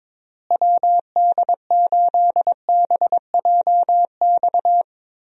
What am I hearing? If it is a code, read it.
Morse code, heard as a keyed tone, about 22 wpm, 700 Hz. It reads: WD8BJX
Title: USA Call Signs - 6 Characters - Repeated 3x - 22wpm